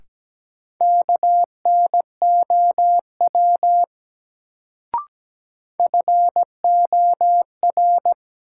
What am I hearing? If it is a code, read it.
Morse code, heard as a keyed tone, about 17 wpm, 700 Hz. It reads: KNOW FOR
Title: Sentences from Top 100 Words - New Word Drill - 17wpm